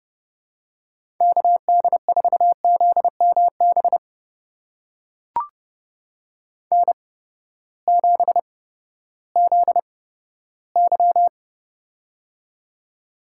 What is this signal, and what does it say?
Morse code, heard as a keyed tone, about 31 wpm, spaced as for 15 wpm, 700 Hz. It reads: KD4ZMB N7ZY
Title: USA Call Signs - 8x Character Spacing - 30wpm